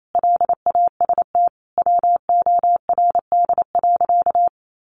Morse code, heard as a keyed tone, about 28 wpm, 700 Hz. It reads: LAST WORD.
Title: Sentences from Top 1200 Words 28wpm